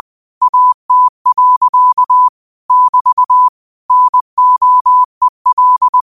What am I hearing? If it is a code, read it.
Morse code, heard as a keyed tone, about 20 wpm, 1000 Hz. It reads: AT. <BT> NOEL
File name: qotdfiles / 20wpm